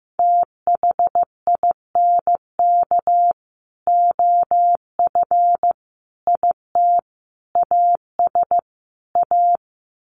Morse code, heard as a keyed tone, about 15 wpm, 700 Hz. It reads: THINK OF IT AS A
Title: Sets of 5 Words Encore2 15wpm